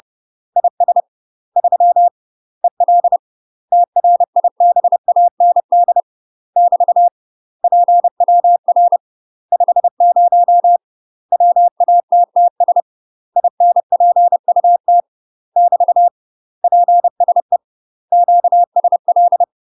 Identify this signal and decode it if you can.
Morse code, heard as a keyed tone, about 30 wpm, 700 Hz. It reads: IS 3 EL TRIBAND <BT> PWR 50 WATTS INPUT <BT> PSE QSL
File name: UK MASTER QSOs TRAINING / 30WPM / UK MASTER QSOs 0036